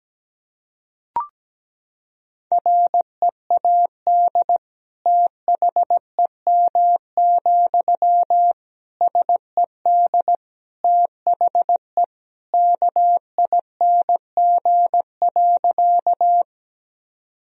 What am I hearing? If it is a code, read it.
Morse code, heard as a keyed tone, about 17 wpm, 700 Hz. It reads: READ THEM, SED THE KING.
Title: Sentences from Top 400 Words and Abbreviations 17wpm